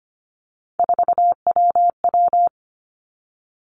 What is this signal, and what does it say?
Morse code, heard as a keyed tone, about 25 wpm, 700 Hz. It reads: 4WW
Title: Three Challenging Letter-Number Combinations 25wpm